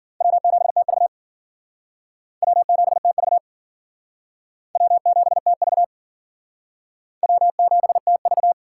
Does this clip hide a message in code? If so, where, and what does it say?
Yes: Morse code from 0.2 to 8.5 s, heard as a keyed tone, about 51 wpm, 700 Hz. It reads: W7TV W7TV W7TV W7TV